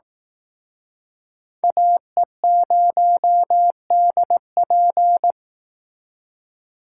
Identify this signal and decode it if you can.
Morse code, heard as a keyed tone, about 22 wpm, 700 Hz. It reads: AE0DP